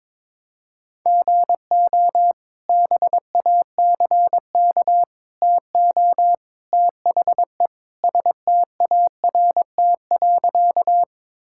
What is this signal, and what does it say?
Morse code, heard as a keyed tone, about 22 wpm, 700 Hz. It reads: GO BACK TO THE START.
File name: practice / 22wpm / Sentences with 3-Word Phrases - 22wpm